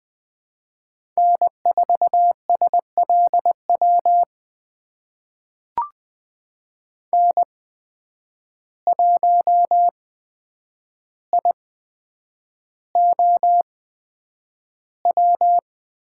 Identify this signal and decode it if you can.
Morse code, heard as a keyed tone, about 20 wpm, spaced as for 9 wpm, 700 Hz. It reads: N4SLW N1IOW
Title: USA Call Signs - 5 Characters - 8x Character Spacing - 20wpm